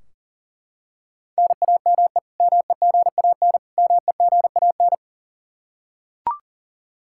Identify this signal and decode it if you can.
Morse code, heard as a keyed tone, about 40 wpm, 700 Hz. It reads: NAME MEGAN MEGAN
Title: Practice QSO - Exchange 1 - Beginning - 40wpm